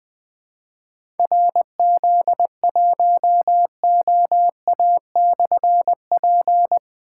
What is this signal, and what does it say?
Morse code, heard as a keyed tone, about 21 wpm, 700 Hz. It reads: RZ1OA/P
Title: Difficult Call Signs 20wpm